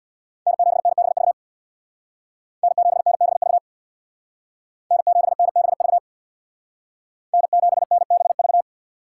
Morse code, heard as a keyed tone, about 63 wpm, 700 Hz. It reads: N7NBV N7NBV N7NBV N7NBV